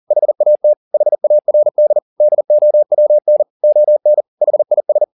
Morse code, heard as a keyed tone, about 40 wpm, 600 Hz. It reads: HAT HARD DOWN ON HIS